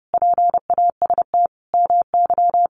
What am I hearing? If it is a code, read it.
Morse code, heard as a keyed tone, about 28 wpm, 700 Hz. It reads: PAST MY
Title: Sentences from Top 1100 Words 30wpm